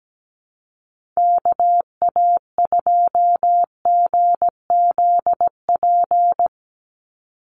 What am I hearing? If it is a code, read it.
Morse code, heard as a keyed tone, about 17 wpm, 700 Hz. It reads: KA2GZP
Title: USA Call Signs 17wpm